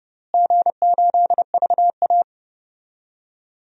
Morse code, heard as a keyed tone, about 30 wpm, 700 Hz. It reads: G8VA
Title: DXCC-100 Call Signs - 4 Characters - Repeated 3x - 30wpm